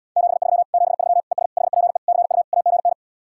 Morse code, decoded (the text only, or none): 73 73 ES HPE CU AGN